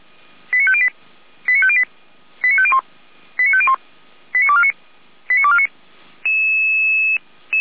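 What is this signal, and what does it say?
Five-tone selective call (ZVEI), heard as quick runs of five tones, tones of about 70 ms. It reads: 89508 89508 89501 89501 89249 89249
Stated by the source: ZVEI five-tone call received by SDR, logged on Feuerwehr Orlamünde audio server.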